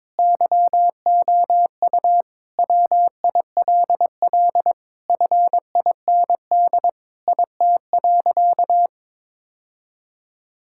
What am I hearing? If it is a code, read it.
Morse code, heard as a keyed tone, about 22 wpm, 700 Hz. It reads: YOU WILL FIND IT.
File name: practice / 22wpm / Sentences with 3-Word Phrases - 22wpm